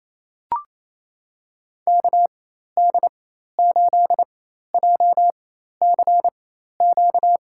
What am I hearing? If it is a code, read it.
Morse code, heard as a keyed tone, about 28 wpm, spaced as for 13 wpm, 700 Hz. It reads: KD8JCQ
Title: USA Call Signs - 6 Characters - 4x Character Spacing - 28wpm